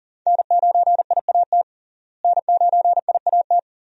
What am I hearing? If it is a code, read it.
Morse code, heard as a keyed tone, about 40 wpm, 700 Hz. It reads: N9IAT N9IAT
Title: USA Call Signs - 5 Characters - Repeated 2x - 40wpm